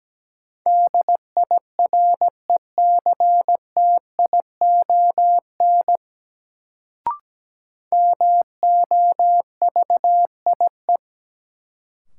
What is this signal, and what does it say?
Morse code, heard as a keyed tone, about 17 wpm, 700 Hz. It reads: DIRECTION MOVIE
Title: Sentences from Top 1000 Words - Review Part 5 - New Word Drill - 17wpm